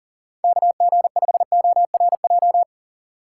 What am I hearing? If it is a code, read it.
Morse code, heard as a keyed tone, about 38 wpm, 700 Hz. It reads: KG5ORJ
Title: USA Call Signs - 6 Characters - 4x Character Spacing - 40wpm